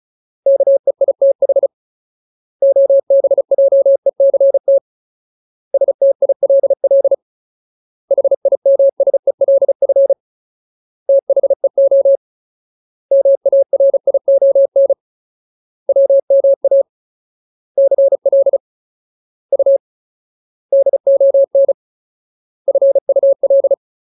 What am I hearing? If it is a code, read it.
Morse code, heard as a keyed tone, about 34 wpm, 550 Hz. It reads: KEITH OBJECT STILL HIMSELF THEO MARION WMA CL U DON FUL
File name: Ultimate Word List / 35WPM Ultimate